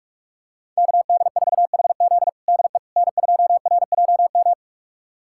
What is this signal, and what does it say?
Morse code, heard as a keyed tone, about 43 wpm, 700 Hz. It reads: KD4HZ DE N2RJM
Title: USA Callsign DE Callsign - 5 Characters - 45wpm